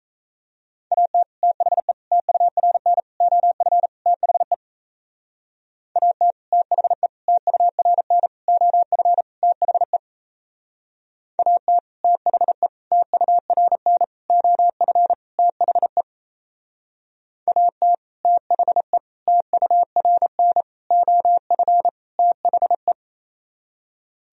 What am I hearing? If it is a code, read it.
Morse code, heard as a keyed tone, about 38 wpm, 700 Hz. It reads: AT THE TURN OF THE AT THE TURN OF THE AT THE TURN OF THE AT THE TURN OF THE